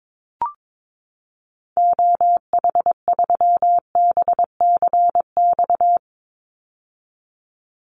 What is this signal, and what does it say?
Morse code, heard as a keyed tone, about 22 wpm, 700 Hz. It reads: OH3BCX
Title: Call Signs - 22wpm